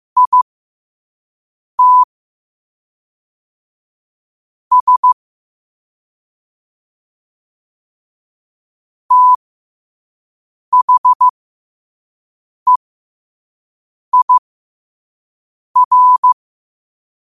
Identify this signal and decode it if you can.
Morse code, heard as a keyed tone, about 15 wpm, spaced as for 5 wpm, 1000 Hz. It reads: IT S THEIR
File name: 5wpm